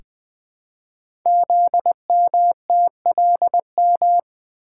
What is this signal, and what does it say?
Morse code, heard as a keyed tone, about 20 wpm, 700 Hz. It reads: ZMTLM
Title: Code Groups - 5 Letters - 20wpm